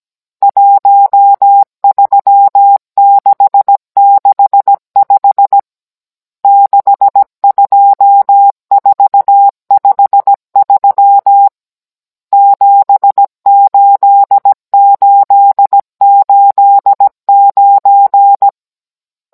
Morse code, heard as a keyed tone, about 17 wpm, 800 Hz. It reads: 13665 62453 78889
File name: Cours-NR19-16mm